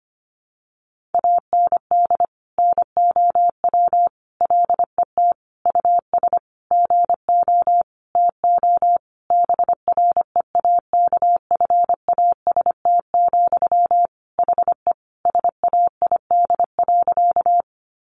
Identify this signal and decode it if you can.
Morse code, heard as a keyed tone, about 25 wpm, 700 Hz. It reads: AND NOW LET US GO TO BREAKFAST, HE SAID.
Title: Sentences from Top 1500 Words 25wpm